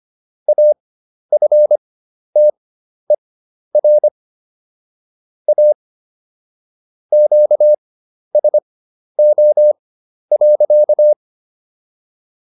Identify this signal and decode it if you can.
Morse code, heard as a keyed tone, about 25 wpm, spaced as for 11 wpm, 600 Hz. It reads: AFTER A QSO.